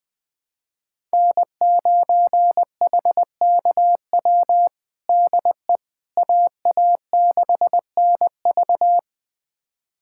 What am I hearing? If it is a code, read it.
Morse code, heard as a keyed tone, about 20 wpm, 700 Hz. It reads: N9HKW DE AA6NV
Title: USA Callsign DE Callsign - 5 Characters - 20wpm